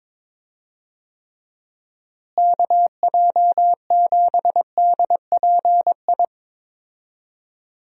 Morse code, heard as a keyed tone, about 21 wpm, 700 Hz. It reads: KJ7DPI
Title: USA Call Signs - 6 Characters - Repeated 3x - 22wpm